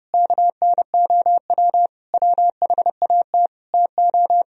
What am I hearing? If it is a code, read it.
Morse code, heard as a keyed tone, about 38 wpm, 700 Hz. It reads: KNOW WHAT TO